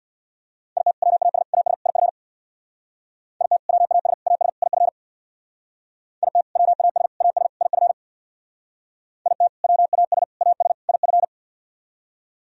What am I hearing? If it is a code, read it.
Morse code, heard as a keyed tone, about 64 wpm, 700 Hz. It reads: IT WAS AS IF IT WAS AS IF IT WAS AS IF IT WAS AS IF